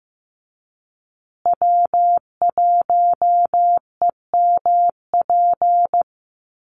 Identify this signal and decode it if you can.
Morse code, heard as a keyed tone, about 15 wpm, 700 Hz. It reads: W1EMP
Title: USA Call Signs Repeated 3x 15wpm